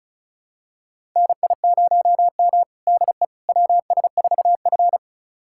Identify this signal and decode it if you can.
Morse code, heard as a keyed tone, about 34 wpm, 700 Hz. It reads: NI0M DE WS4F
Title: USA Callsign DE Callsign - 4 Characters - 35wpm